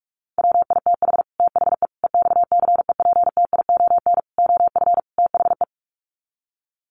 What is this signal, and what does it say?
Morse code, heard as a keyed tone, about 44 wpm, 700 Hz. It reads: WITH THE EXCEPTION OF THE
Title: Sets of 5 Words Encore 45wpm